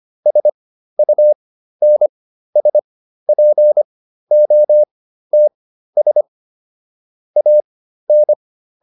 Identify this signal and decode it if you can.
Morse code, heard as a keyed tone, about 25 wpm, spaced as for 13 wpm, 600 Hz. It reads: SUNSPOTS AN